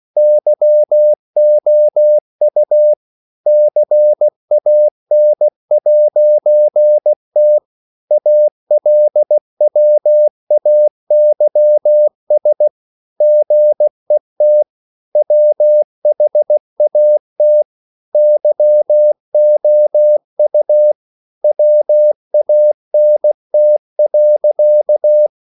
Morse code, heard as a keyed tone, about 16 wpm, 600 Hz. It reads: YOU CAN'T ALWAYS GET WHAT YOU WANT.
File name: CW/Practice